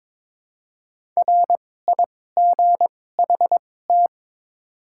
Morse code, heard as a keyed tone, about 22 wpm, spaced as for 16 wpm, 700 Hz. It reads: RIGHT
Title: 2x Character Spacing - Top 200 Words - 22wpm